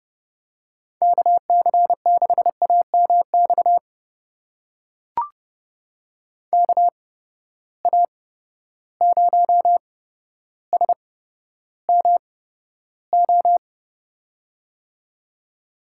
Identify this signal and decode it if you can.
Morse code, heard as a keyed tone, about 31 wpm, spaced as for 13 wpm, 700 Hz. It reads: KC6AMX KA0SMO
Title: USA Call Signs - 6 Characters - 8x Character Spacing - 30wpm